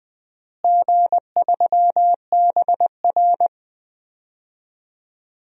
Morse code, heard as a keyed tone, about 21 wpm, 700 Hz. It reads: G3BR
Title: DXCC-100 Call Signs - 4 Characters - Repeated 2x - 20wpm